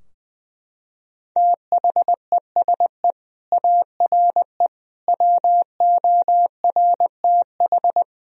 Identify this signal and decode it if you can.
Morse code, heard as a keyed tone, about 20 wpm, 700 Hz. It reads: THESE ARE WORTH
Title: Sentences from Top 1000 Words - Review Part 3 - 20wpm